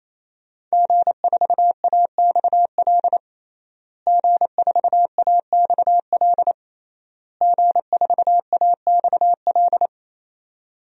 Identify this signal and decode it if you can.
Morse code, heard as a keyed tone, about 28 wpm, 700 Hz. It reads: G4AXL G4AXL G4AXL
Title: DXCC-100 Call Signs - 5 Characters - Repeated 3x - 28wpm